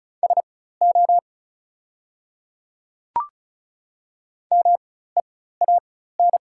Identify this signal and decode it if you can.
Morse code, heard as a keyed tone, about 34 wpm, spaced as for 16 wpm, 690 Hz. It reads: SO MEAN
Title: 4x Character Spacing - Top 200 Words - 35wpm